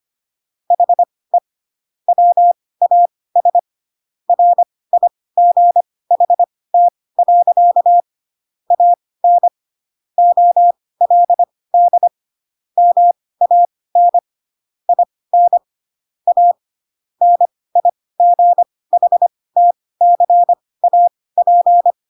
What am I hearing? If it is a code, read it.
Morse code, heard as a keyed tone, about 25 wpm, spaced as for 18 wpm, 700 Hz. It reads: HE WAS RIGHT. AN OLD MAN IN A NIGHTCAP
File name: Pinocchio / Pinocchio ch6 12 25 ik1hge